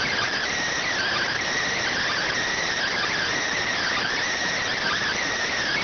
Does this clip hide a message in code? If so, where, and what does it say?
Yes: Emergency Alert System (SAME) tones (header bursts) from 0.9 to 5.2 s, heard as harsh buzzing bursts.